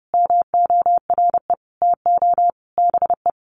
Morse code, heard as a keyed tone, about 30 wpm, 700 Hz. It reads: MORE TO BE
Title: Sentences from Top 1000 Words 30wpm